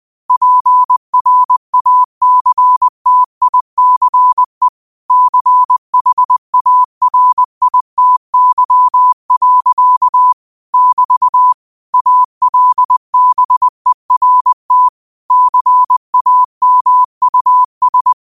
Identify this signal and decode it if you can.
Morse code, heard as a keyed tone, about 20 wpm, 1000 Hz. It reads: PRACTICE CHARITY. <BT> ALBERT CAMUS
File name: qotdfiles / 20wpm